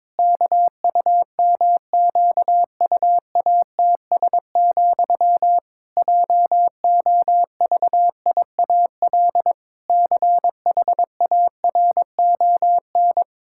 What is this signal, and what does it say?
Morse code, heard as a keyed tone, about 22 wpm, 700 Hz. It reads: KUMQUATS, JOVIAL CHARON
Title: Pangram Challenge v2 - 22wpm